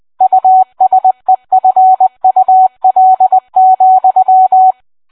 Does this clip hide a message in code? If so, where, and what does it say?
Yes: Morse code from 0.2 to 4.7 s, heard as a keyed tone, about 20 wpm, 750 Hz, with radio noise. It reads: USEFUL,